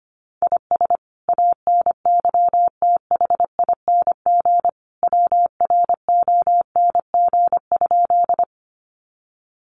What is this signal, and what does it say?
Morse code, heard as a keyed tone, about 25 wpm, 700 Hz. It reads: IS ANYTHING WRONG?
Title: Sentences from Top 1000 Words 25wpm